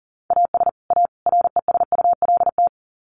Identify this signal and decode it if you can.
Morse code, heard as a keyed tone, about 40 wpm, 700 Hz. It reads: AS A RESULT
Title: Sets of 3 Words 40wpm